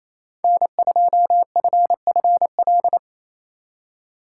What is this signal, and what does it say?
Morse code, heard as a keyed tone, about 36 wpm, 700 Hz. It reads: N2FFL